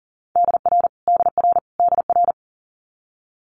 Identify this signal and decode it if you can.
Morse code, heard as a keyed tone, about 37 wpm, 700 Hz. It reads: DR DR DR
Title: Multi-letter Phonemes Repeated 3x 40wpm